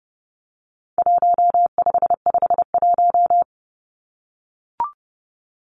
Morse code, heard as a keyed tone, about 30 wpm, 700 Hz. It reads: 1551
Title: Numbers - Contest Style - 4-Digit 30wpm